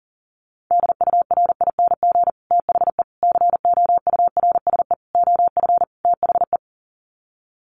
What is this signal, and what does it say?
Morse code, heard as a keyed tone, about 40 wpm, 700 Hz. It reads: DURING THE COURSE OF THE
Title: Sets of 5 Words Encore 40wpm